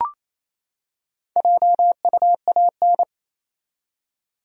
Morse code, heard as a keyed tone, about 28 wpm, 700 Hz. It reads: JUAN
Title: USA Common Names - Max 5 Letters - 28wpm